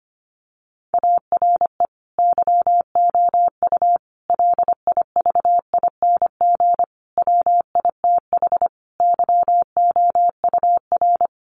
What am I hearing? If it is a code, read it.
Morse code, heard as a keyed tone, about 25 wpm, 700 Hz. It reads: ARE YOU LIVING WITH YOUR
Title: Sentences from Top 1100 Words 25wpm